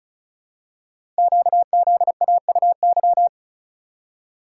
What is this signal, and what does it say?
Morse code, heard as a keyed tone, about 36 wpm, 690 Hz. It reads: QZAUY
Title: Code Groups - 5 Letters - 35wpm